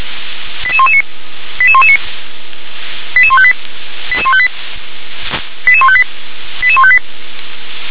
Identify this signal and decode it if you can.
Five-tone selective call (ZVEI), heard as quick runs of five tones, tones of about 70 ms. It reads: 88119 88119 88158 88158 88158 88158